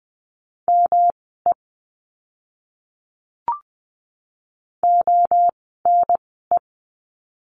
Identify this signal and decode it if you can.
Morse code, heard as a keyed tone, about 20 wpm, spaced as for 15 wpm, 700 Hz. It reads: ME ONE
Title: 230 Lesson 37f - Letter x - Words 20wpm